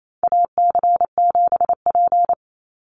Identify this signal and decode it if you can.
Morse code, heard as a keyed tone, about 28 wpm, 700 Hz. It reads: AC7P
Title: NA SOTA Chaser Callsigns 28wpm